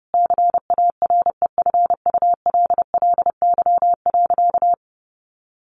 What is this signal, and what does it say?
Morse code, heard as a keyed tone, about 30 wpm, 700 Hz. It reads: CAREFULLY.
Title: Sentences from Top 1500 Words 30wpm